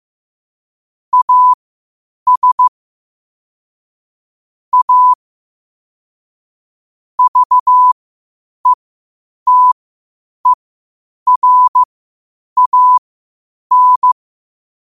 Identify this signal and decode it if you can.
Morse code, heard as a keyed tone, about 15 wpm, spaced as for 8 wpm, 1000 Hz. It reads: AS A VETERAN
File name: qotdfiles / 8wpm